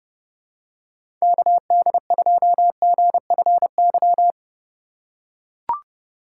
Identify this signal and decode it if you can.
Morse code, heard as a keyed tone, about 30 wpm, 700 Hz. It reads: KD2GFY
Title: USA Call Signs - 6 Characters - 2x Character Spacing - 30wpm